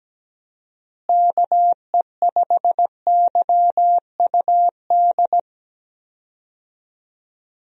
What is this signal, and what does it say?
Morse code, heard as a keyed tone, about 17 wpm, 700 Hz. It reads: KE5YUD
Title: USA Call Signs - 6 Characters - Repeated 2x - 17wpm